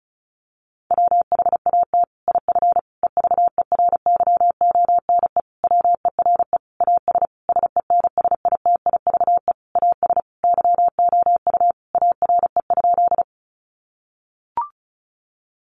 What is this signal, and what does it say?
Morse code, heard as a keyed tone, about 34 wpm, 700 Hz. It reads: WHAT IF EVERYONE WERE AS SENSITIVE AS YOU ARE?
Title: Sentences from Top 1400 Words 35wpm